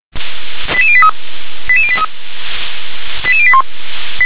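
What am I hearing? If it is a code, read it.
Five-tone selective call (ZVEI), heard as quick runs of five tones, tones of about 80 ms. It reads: 88083 88083 88081
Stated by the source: ZVEI five-tone selective call received by SDR.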